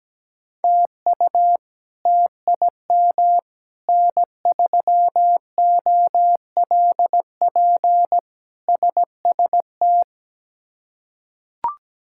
Morse code, heard as a keyed tone, about 17 wpm, 700 Hz. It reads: TU TIM N3OLP SST
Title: K1USN SST QSO Practice - 17wpm